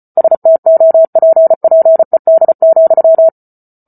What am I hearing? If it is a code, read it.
Morse code, heard as a keyed tone, about 34 wpm, 650 Hz. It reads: STOPPED,